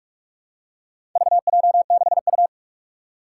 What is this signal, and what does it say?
Morse code, heard as a keyed tone, about 44 wpm, 700 Hz. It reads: VJ6U
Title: Four Challenging Letter-Number Combinations 45wpm